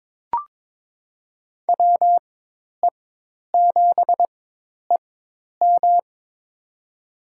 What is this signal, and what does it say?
Morse code, heard as a keyed tone, about 22 wpm, spaced as for 10 wpm, 700 Hz. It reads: WE7EM
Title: USA Call Signs - 5 Characters - 4x Character Spacing - 22wpm